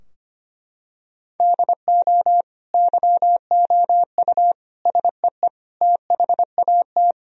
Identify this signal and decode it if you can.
Morse code, heard as a keyed tone, about 25 wpm, 700 Hz. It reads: DO YOU SEE THAT
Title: Sentences from Top 1000 Words - Review Part 3 - 25wpm